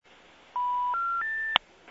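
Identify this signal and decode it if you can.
Special information tone.
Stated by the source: International telephone call progress tone.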